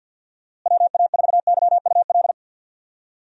Morse code, heard as a keyed tone, about 50 wpm, 700 Hz. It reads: WA4YUL
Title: USA Call Signs - 6 Characters - Repeated 3x - 50wpm